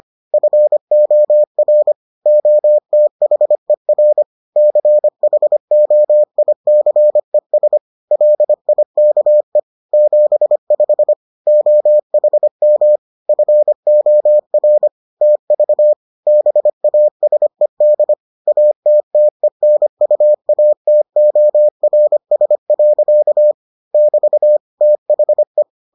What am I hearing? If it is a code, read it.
Morse code, heard as a keyed tone, about 25 wpm, 600 Hz. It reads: FOR OTHER CHOICES LIKE 75 OHM FOR TV BASED ATTENUATORS. <BT> THE